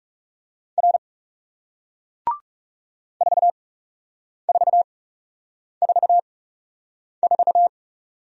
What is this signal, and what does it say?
Morse code, heard as a keyed tone, about 38 wpm, spaced as for 8 wpm, 700 Hz. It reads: R 4444